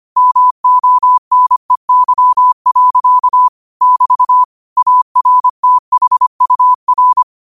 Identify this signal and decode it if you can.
Morse code, heard as a keyed tone, about 25 wpm, 1000 Hz. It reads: MONEY. <BT> ARTHUR
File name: qotdfiles / 25wpm